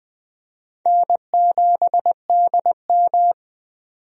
Morse code, heard as a keyed tone, about 20 wpm, 700 Hz. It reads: N7DM
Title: USA Call Signs - 4 Characters - 20wpm